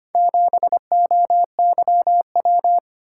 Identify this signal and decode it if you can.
Morse code, heard as a keyed tone, about 25 wpm, 700 Hz. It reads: 7OYW
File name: practice / 25wpm / Four Letter-Number Combinations - 25wpm